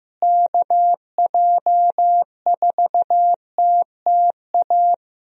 Morse code, heard as a keyed tone, about 15 wpm, 700 Hz. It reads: KJ4TTA
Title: USA Call Signs - 6 Characters - Repeated 3x - 15wpm